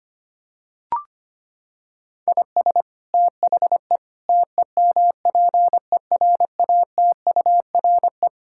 Morse code, heard as a keyed tone, about 25 wpm, 700 Hz. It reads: IS THE TEMPERATURE
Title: Sentences from Top 1000 Words - Review Part 6 - 25wpm